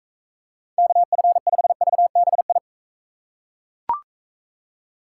Morse code, heard as a keyed tone, about 38 wpm, 700 Hz. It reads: KF5VBI